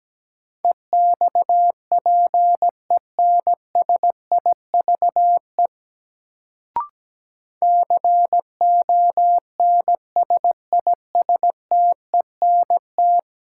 Morse code, heard as a keyed tone, about 17 wpm, 700 Hz. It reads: EXPENSIVE CONSISTENT
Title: Sentences from Top 1500 Words - New Word Drill Alt - 17wpm